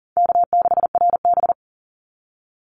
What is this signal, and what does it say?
Morse code, heard as a keyed tone, about 37 wpm, 700 Hz. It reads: K6RB
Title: Call Signs Repeated 2x - 40wpm